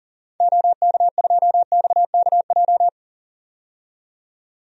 Morse code, heard as a keyed tone, about 40 wpm, 700 Hz. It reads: OK2XKJ
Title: DXCC-100 Call Signs - 6 Characters - 40wpm